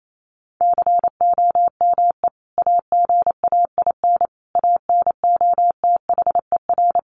Morse code, heard as a keyed tone, about 28 wpm, 700 Hz. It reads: COME AGAIN ANOTHER
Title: Sentences from Top 600 Words 28wpm